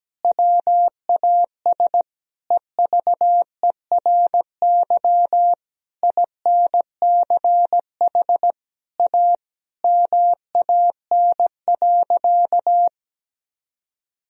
Morse code, heard as a keyed tone, about 17 wpm, 700 Hz. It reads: WAS EVERY INCH A MAN.
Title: Sentences from Top 1000 Words - Review Part 3 - 17wpm